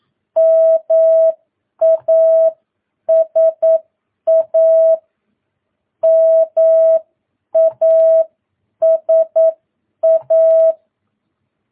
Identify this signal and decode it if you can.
Morse code, heard as a keyed tone, about 9 wpm, spaced as for 8 wpm, 650 Hz. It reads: MASA MASA
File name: jh8cbh / 2022cw